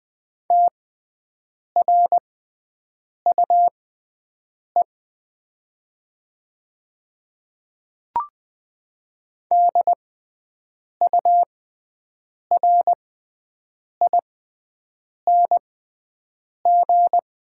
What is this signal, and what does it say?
Morse code, heard as a keyed tone, about 20 wpm, spaced as for 7 wpm, 700 Hz. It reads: TRUE DURING
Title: 6x Character Spacing - Top 500 Words - 20wpm